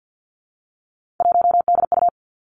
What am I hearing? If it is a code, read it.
Morse code, heard as a keyed tone, about 50 wpm, 700 Hz. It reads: 1DU
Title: Three Challenging Letter-Number Combinations 50wpm